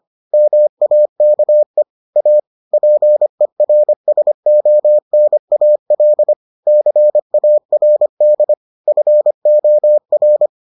Morse code, heard as a keyed tone, about 25 wpm, 600 Hz. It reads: MAKE A PERSONAL CARD FOR